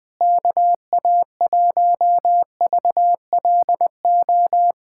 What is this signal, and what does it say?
Morse code, heard as a keyed tone, about 20 wpm, 700 Hz. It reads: KA1VLO
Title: USA Callsign DE Callsign - 6 Characters - 20wpm